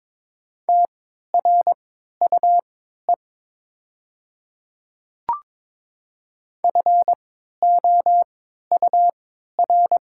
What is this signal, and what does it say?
Morse code, heard as a keyed tone, about 22 wpm, spaced as for 13 wpm, 700 Hz. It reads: TRUE FOUR
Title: 3x Character Spacing - Top 500 Words - 22wpm